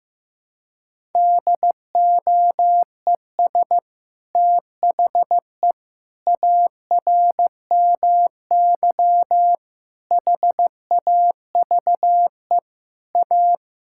Morse code, heard as a keyed tone, about 15 wpm, 700 Hz. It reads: DOES THE ARMY HAVE A
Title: Sentences from Top 1000 Words - Review Part 5 - 15wpm